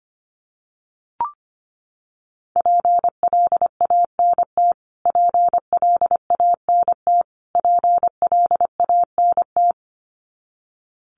Morse code, heard as a keyed tone, about 25 wpm, 700 Hz. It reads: PLANT PLANT PLANT
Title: Top 500 Words Repeated 25wpm